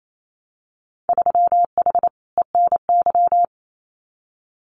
Morse code, heard as a keyed tone, about 28 wpm, 700 Hz. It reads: 3H ENY
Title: Winter Field Day Exchanges Encore 28wpm